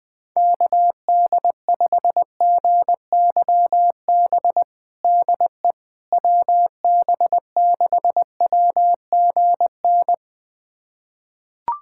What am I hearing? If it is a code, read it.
Morse code, heard as a keyed tone, about 21 wpm, 700 Hz. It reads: KD5GYB DE WB6WGN
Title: USA Callsign DE Callsign - 6 Characters - 20wpm